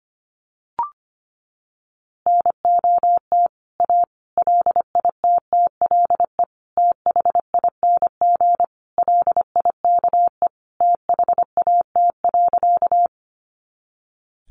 Morse code, heard as a keyed tone, about 25 wpm, 700 Hz. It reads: NOT A LITTLE THING LIKE THAT.
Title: A Sound Of Thunder 25wpm